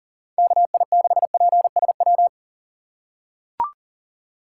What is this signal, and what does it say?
Morse code, heard as a keyed tone, about 40 wpm, 700 Hz. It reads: KI6PSW
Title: USA Call Signs - 6 Characters - 3x Character Spacing - 40wpm